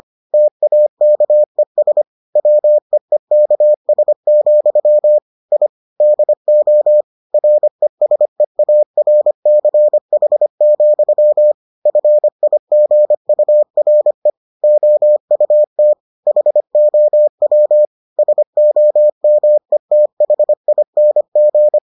Morse code, heard as a keyed tone, about 25 wpm, 600 Hz. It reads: TAKES WEEKS, I DO RESEARCH, FIGURE OUT HOW SOMETHING